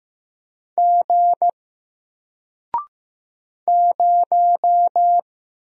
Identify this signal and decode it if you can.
Morse code, heard as a keyed tone, about 15 wpm, 700 Hz. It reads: G 0
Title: Single Character (Letter:Number:Punctuation:Prosign) Alt 15wpm